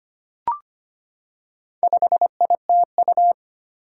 Morse code, heard as a keyed tone, about 25 wpm, 700 Hz. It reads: 5ITU
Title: Four Challenging Letter-Number Combinations 25wpm